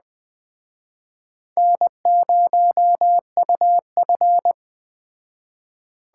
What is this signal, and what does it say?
Morse code, heard as a keyed tone, about 21 wpm, 700 Hz. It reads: N0UF
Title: Speed Racing - USA 1x2 Call Signs 17wpm